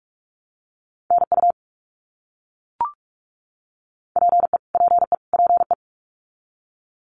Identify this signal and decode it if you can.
Morse code, heard as a keyed tone, about 44 wpm, 700 Hz. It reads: NU PE PE PE
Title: Canadian Provinces and Territories - Repeated 3x - 45wpm